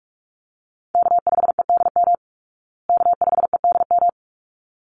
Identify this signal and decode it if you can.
Morse code, heard as a keyed tone, about 44 wpm, 700 Hz. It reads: K5EDM K5EDM
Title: Call Signs Repeated 3x - 45wpm